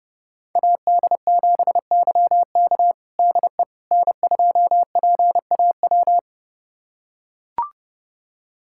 Morse code, heard as a keyed tone, about 30 wpm, 700 Hz. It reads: AD7YK DE N2PAW
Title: USA Callsign DE Callsign - 5 Characters - 30wpm